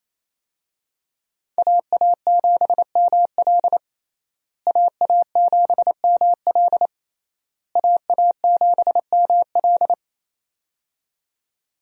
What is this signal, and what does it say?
Morse code, heard as a keyed tone, about 28 wpm, 700 Hz. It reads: AA7ML AA7ML AA7ML
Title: USA Call Signs - 5 Characters - Repeated 3x - 28wpm